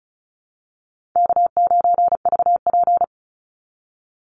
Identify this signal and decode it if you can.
Morse code, heard as a keyed tone, about 34 wpm, 700 Hz. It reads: K9VP
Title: Easy Call Signs Repeated 2x - 35wpm